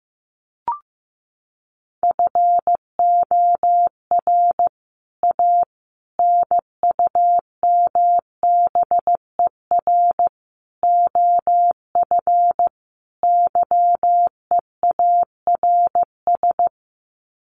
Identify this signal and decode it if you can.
Morse code, heard as a keyed tone, about 15 wpm, 700 Hz. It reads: FOR A NUMBER OF YEARS
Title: Sets of 5 Words Encore 15wpm